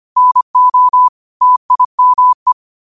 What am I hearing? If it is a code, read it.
Morse code, heard as a keyed tone, about 25 wpm, 1000 Hz. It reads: NO TIME
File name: qotdfiles / 25wpm